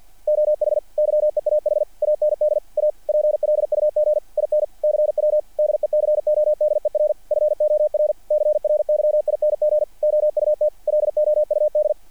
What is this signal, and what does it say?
Morse code, heard as a keyed tone, about 48 wpm, 600 Hz. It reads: OH YEAH AND A PLUG IN CW DECODER FOR CRYING OUT LOUD
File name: archived / CRSnet